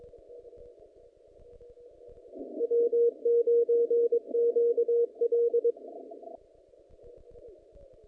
Morse code, heard as a keyed tone, about 22 wpm, 470 Hz. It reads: W9QL
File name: contests / naqp-cw-k5tr-op-k5pi- -0300